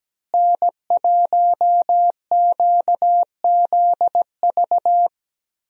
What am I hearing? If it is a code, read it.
Morse code, heard as a keyed tone, about 17 wpm, 700 Hz. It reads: N1QZV
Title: USA Call Signs - 5 Characters - Repeated 2x - 17wpm